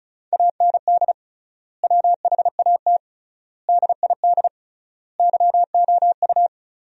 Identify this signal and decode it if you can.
Morse code, heard as a keyed tone, about 36 wpm, 700 Hz. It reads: AND WHAT DID YOU
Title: 3x Word-Spacing - Sentences from Top 100 Words - 35wpm